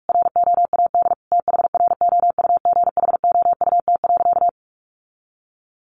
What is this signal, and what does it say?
Morse code, heard as a keyed tone, about 44 wpm, 700 Hz. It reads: ROAD THROUGHOUT.
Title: Sentences from Top 800 Words 45wpm